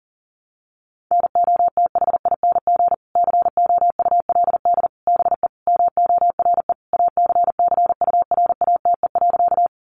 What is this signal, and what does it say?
Morse code, heard as a keyed tone, about 40 wpm, 700 Hz. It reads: NOTHING COULD BE MORE ACCURATE.
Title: Sentences from Top 1200 Words 40wpm